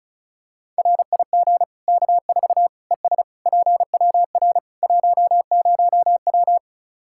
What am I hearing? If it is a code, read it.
Morse code, heard as a keyed tone, about 36 wpm, 700 Hz. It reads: RIG K4 ES PWR 10W
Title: Practice QSO - Exchange 2 - Info - 35wpm